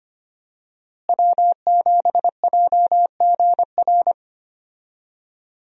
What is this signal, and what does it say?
Morse code, heard as a keyed tone, about 25 wpm, 700 Hz. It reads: W7JGR
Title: USA Call Signs - 5 Characters - Repeated 3x - 25wpm